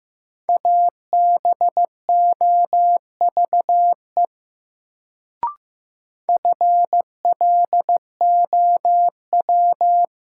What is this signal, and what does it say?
Morse code, heard as a keyed tone, about 15 wpm, 700 Hz. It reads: ABOVE FLOW
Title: Sentences from Top 1200 Words - New Word Drill Alt - 15wpm